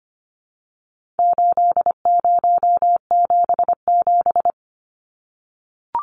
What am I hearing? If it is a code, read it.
Morse code, heard as a keyed tone, about 25 wpm, 700 Hz. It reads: 8077
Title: Numbers - 4-Digit 25wpm